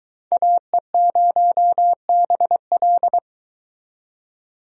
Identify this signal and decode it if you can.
Morse code, heard as a keyed tone, about 22 wpm, 700 Hz. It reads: AE0BL